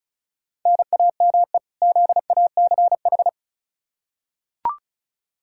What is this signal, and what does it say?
Morse code, heard as a keyed tone, about 36 wpm, 700 Hz. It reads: NAME ZACH
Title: K1USN SST QSO Practice 35wpm